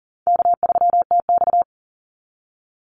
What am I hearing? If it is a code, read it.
Morse code, heard as a keyed tone, about 40 wpm, 700 Hz. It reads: K3TX
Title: Easy Call Signs Repeated 2x - 40wpm